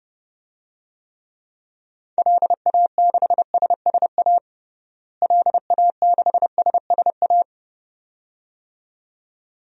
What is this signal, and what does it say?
Morse code, heard as a keyed tone, about 30 wpm, 700 Hz. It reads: LA6SSA LA6SSA
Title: DXCC-100 Call Signs - 6 Characters - Repeated 2x - 30wpm